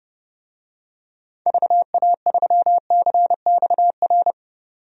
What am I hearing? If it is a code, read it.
Morse code, heard as a keyed tone, about 30 wpm, 700 Hz. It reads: VA3CXR
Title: DXCC-100 Call Signs - 6 Characters - 30wpm